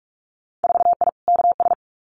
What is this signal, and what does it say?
Morse code, heard as a keyed tone, about 46 wpm, 700 Hz. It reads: 4I KS
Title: Winter Field Day Exchanges Encore 45wpm